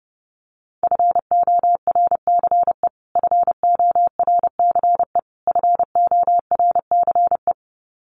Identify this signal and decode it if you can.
Morse code, heard as a keyed tone, about 30 wpm, 700 Hz. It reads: FORCE FORCE FORCE
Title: Top 500 Words Repeated 30wpm